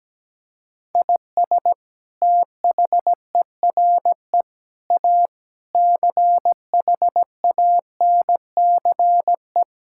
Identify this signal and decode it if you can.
Morse code, heard as a keyed tone, about 17 wpm, 700 Hz. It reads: IS THERE A CHANCE
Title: Sentences from Top 1000 Words - Review Part 6 - 17wpm